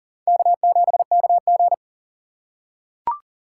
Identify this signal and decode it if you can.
Morse code, heard as a keyed tone, about 37 wpm, 700 Hz. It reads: K7KG
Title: USA Call Signs - 4 Characters - 3x Character Spacing - 40wpm